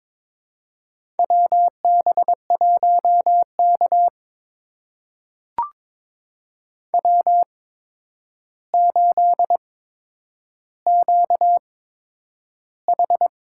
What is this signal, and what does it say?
Morse code, heard as a keyed tone, about 22 wpm, spaced as for 9 wpm, 700 Hz. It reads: WB1K W8QH
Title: USA Call Signs - 4 Characters - 8x Character Spacing - 22wpm